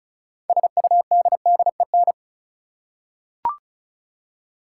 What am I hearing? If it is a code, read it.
Morse code, heard as a keyed tone, about 36 wpm, 700 Hz. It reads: SUDDEN
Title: Six Letter Words - 35wpm